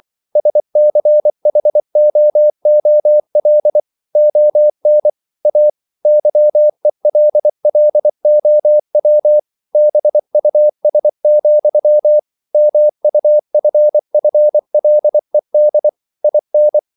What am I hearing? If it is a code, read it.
Morse code, heard as a keyed tone, about 24 wpm, 600 Hz. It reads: SCHOOL ON A YELLOW BUS, MUFFLED IN